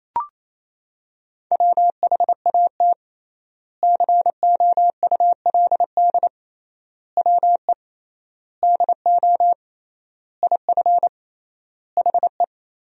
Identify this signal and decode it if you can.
Morse code, heard as a keyed tone, about 28 wpm, 700 Hz. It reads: WHAT COULD WE DO IF HE
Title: 3x Word-Spacing - Sentences from Top 400 Words - 28wpm